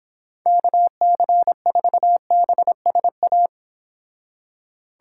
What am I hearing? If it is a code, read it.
Morse code, heard as a keyed tone, about 26 wpm, 700 Hz. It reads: KC4BSA
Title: Speed Racing - USA Call Signs 22wpm